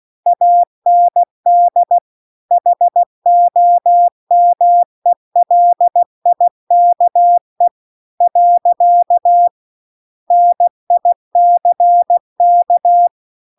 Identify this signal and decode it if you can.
Morse code, heard as a keyed tone, about 16 wpm, 700 Hz. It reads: AND HOMELIKE . NICK